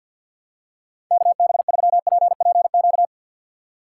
Morse code, heard as a keyed tone, about 53 wpm, 700 Hz. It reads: KB3PPQ
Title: USA Call Signs - 6 Characters - 6x Character Spacing - 50wpm